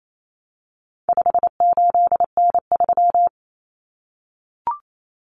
Morse code, heard as a keyed tone, about 28 wpm, 700 Hz. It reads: 58N3
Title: Common Cut Numbers - 4-Digit 28wpm